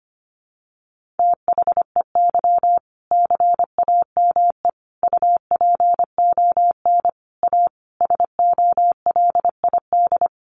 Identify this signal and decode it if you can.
Morse code, heard as a keyed tone, about 25 wpm, 700 Hz. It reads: THEY CAME UPON A SOLID
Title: Sentences from Top 1500 Words 25wpm